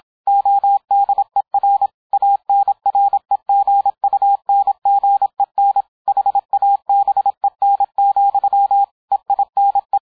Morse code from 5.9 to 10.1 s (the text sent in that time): HABEN, EINE